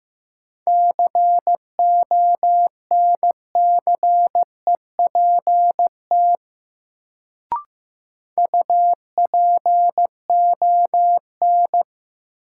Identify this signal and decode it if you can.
Morse code, heard as a keyed tone, about 15 wpm, 700 Hz. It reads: CONCEPT UPON
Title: Sentences from Top 1100 Words - New Word Drill Alt - 15wpm